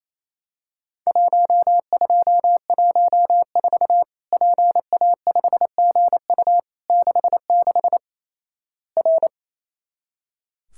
Morse code, heard as a keyed tone, about 28 wpm, 700 Hz. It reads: 1214 PA5GU 66 R
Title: Worked All Europe DX Contest - QTC Exchange - 28wpm